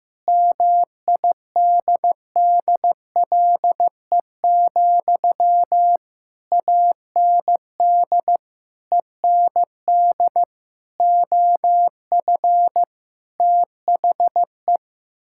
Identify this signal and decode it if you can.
Morse code, heard as a keyed tone, about 15 wpm, 700 Hz. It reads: MIDDLE, AND END OF THE
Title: Sentences from Top 1000 Words - Review Part 6 - 15wpm